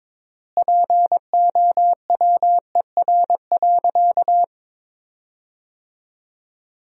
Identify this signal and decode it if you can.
Morse code, heard as a keyed tone, about 22 wpm, 700 Hz. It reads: POWER.
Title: 3x Word-Spacing - Sentences from Top 500 Words - 22wpm